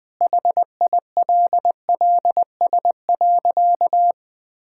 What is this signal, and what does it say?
Morse code, heard as a keyed tone, about 20 wpm, 700 Hz. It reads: HILLS.
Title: Sentences with 2-Word Phrases - 20wpm